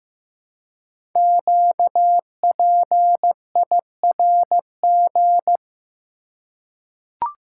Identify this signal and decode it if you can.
Morse code, heard as a keyed tone, about 15 wpm, 700 Hz. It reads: QPIRG
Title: Code Groups - 5 Letters - 15wpm